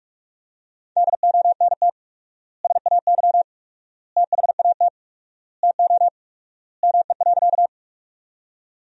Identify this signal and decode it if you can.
Morse code, heard as a keyed tone, about 46 wpm, 700 Hz. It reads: DONT SAY THAT TO ME.
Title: 4x Word-Spacing - Sentences from Top 300 Words - 45wpm